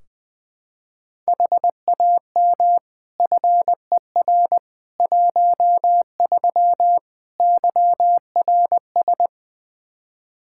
Morse code, heard as a keyed tone, about 21 wpm, 700 Hz. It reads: HAM FER 13 YRS
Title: K1USN SST QSO Practice 20wpm